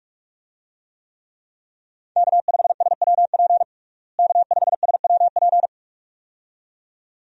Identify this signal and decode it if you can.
Morse code, heard as a keyed tone, about 46 wpm, 700 Hz. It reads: K5SWP K5SWP
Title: USA Call Signs - 5 Characters - Repeated 2x - 45wpm